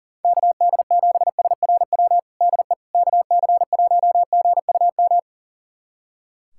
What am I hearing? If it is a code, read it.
Morse code, heard as a keyed tone, about 40 wpm, 700 Hz. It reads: KD7SRW DE KC1GUM
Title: USA Callsign DE Callsign - 6 Characters - 40wpm